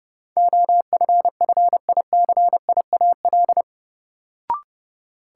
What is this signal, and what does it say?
Morse code, heard as a keyed tone, about 31 wpm, 700 Hz. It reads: OFFICIAL
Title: Sentences from Top 1000 Words - New Word Drill - 30wpm